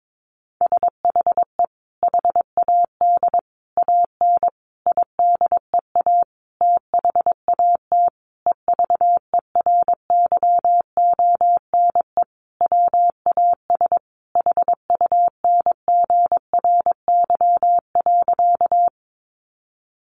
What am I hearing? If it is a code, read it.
Morse code, heard as a keyed tone, about 22 wpm, 700 Hz. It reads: SHE HAD AN IDEA THAT EVERYONE WAS HUNGRY.
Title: Sentences from Top 1400 Words 22wpm